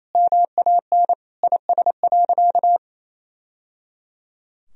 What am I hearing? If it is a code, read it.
Morse code, heard as a keyed tone, about 28 wpm, 700 Hz. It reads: MAN IS.
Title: Sentences from Top 1000 Words - Review Part 2 - 28wpm